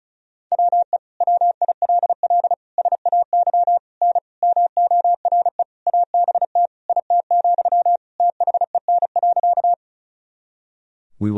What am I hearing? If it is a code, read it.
Morse code, heard as a keyed tone, about 34 wpm, 700 Hz. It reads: WE WILL SAY N MORE ABT IT, THEN.
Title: Sentences from Top 200 Words and Abbreviations 35wpm